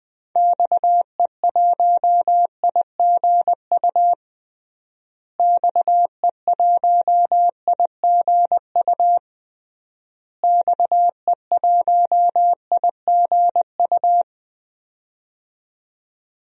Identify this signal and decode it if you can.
Morse code, heard as a keyed tone, about 20 wpm, 700 Hz. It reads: XE1IGU XE1IGU XE1IGU
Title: DXCC-100 Call Signs - 6 Characters - Repeated 3x - 20wpm